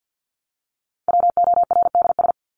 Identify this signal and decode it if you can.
Morse code, heard as a keyed tone, about 48 wpm, 700 Hz. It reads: WORDS
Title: ES:S Suffix - Top 100 Words with Base Words 50wpm